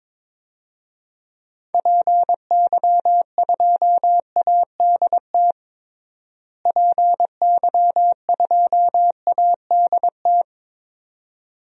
Morse code, heard as a keyed tone, about 22 wpm, 700 Hz. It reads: PY2ADT PY2ADT
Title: DXCC-100 Call Signs - 6 Characters - Repeated 2x - 22wpm